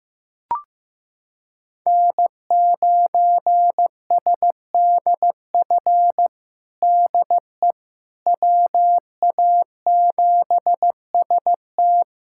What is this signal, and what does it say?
Morse code, heard as a keyed tone, about 15 wpm, 700 Hz. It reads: N9SDF DE WA7ST
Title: USA Callsign DE Callsign - 5 Characters - 15wpm